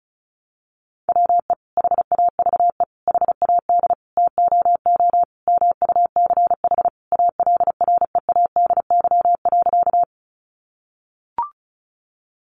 Morse code, heard as a keyed tone, about 34 wpm, 700 Hz. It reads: WE HAVE HAD TOO MUCH ALREADY.
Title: Sentences from Top 700 Words 35wpm